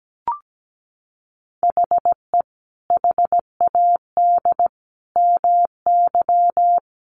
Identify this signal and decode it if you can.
Morse code, heard as a keyed tone, about 17 wpm, 700 Hz. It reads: HE HAD MY
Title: Sentences from Top 400 Words 17wpm